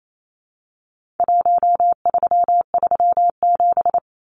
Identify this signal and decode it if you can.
Morse code, heard as a keyed tone, about 28 wpm, 700 Hz. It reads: 1337
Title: Numbers - 4-Digit 28wpm